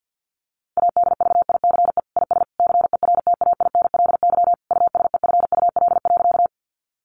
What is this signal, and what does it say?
Morse code, heard as a keyed tone, about 50 wpm, 700 Hz. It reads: ADVICE IS CERTAINLY USEFUL.
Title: Sentences from Top 1100 Words 50wpm